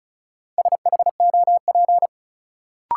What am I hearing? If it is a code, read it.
Morse code, heard as a keyed tone, about 36 wpm, 700 Hz. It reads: SHOP
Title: Sentences from Top 1600 Words - New Word Drill - 35wpm